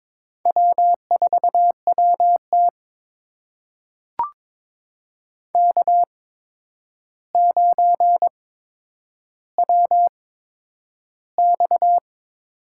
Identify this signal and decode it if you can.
Morse code, heard as a keyed tone, about 22 wpm, spaced as for 9 wpm, 700 Hz. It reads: W4WT K9WX
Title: USA Call Signs - 4 Characters - 8x Character Spacing - 22wpm